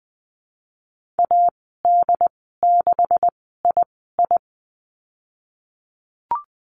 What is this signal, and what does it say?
Morse code, heard as a keyed tone, about 20 wpm, spaced as for 15 wpm, 700 Hz. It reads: AD6II
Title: 147 Lesson 24g - Letter y - Call Signs 20wpm